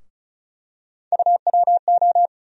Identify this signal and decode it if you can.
Morse code, heard as a keyed tone, about 36 wpm, 700 Hz. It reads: UWO
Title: USA Call Sign Postfixes - 3 Letters 35wpm